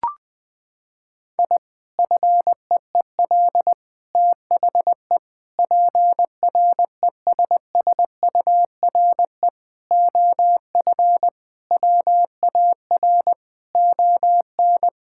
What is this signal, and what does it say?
Morse code, heard as a keyed tone, about 20 wpm, 700 Hz. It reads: I FEEL THE PRESSURE OF WAR ON
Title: Sentences from Top 1000 Words - Review Part 6 - 20wpm